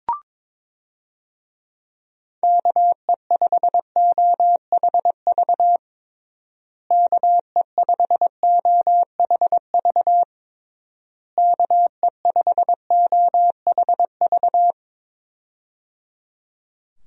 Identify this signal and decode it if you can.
Morse code, heard as a keyed tone, about 22 wpm, 700 Hz. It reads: KE5OHV KE5OHV KE5OHV
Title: USA Call Signs - 6 Characters - Repeated 3x - 22wpm